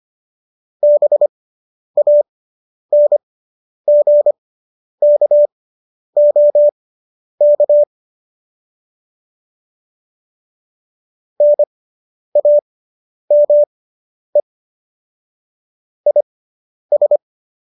Morse code, heard as a keyed tone, about 25 wpm, spaced as for 10 wpm, 590 Hz. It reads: BANGKOK NAME IS
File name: qso005 10